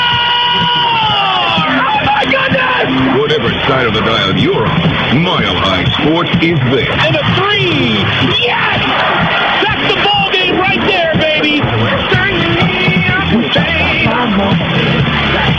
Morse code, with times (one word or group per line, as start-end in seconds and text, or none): none